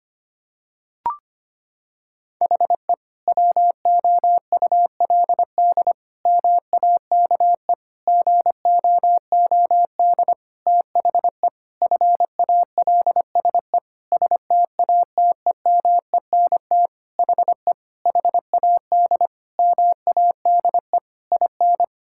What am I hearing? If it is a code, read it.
Morse code, heard as a keyed tone, about 25 wpm, 700 Hz. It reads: HE WOULD MAKE GOOD THE FALSE STATEMENT HE HAD MADE IN
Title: Sentences from Top 1600 Words 25wpm